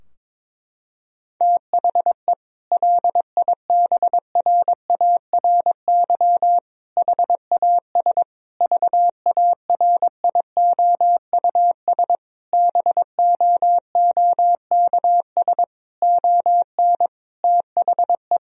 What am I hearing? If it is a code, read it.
Morse code, heard as a keyed tone, about 22 wpm, 700 Hz. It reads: THE LIBRARY HAS VARIOUS BOOKS ON THE
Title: Sentences from Top 1000 Words - Review Part 4 - 22wpm